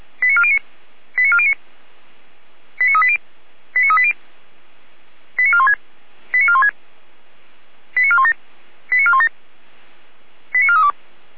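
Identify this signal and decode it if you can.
Five-tone selective call (ZVEI), heard as quick runs of five tones, tones of about 70 ms. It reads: 89409 89409 89380 89380 89416 89416 89417 89417 89432
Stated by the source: ZVEI five-tone call received on SDR.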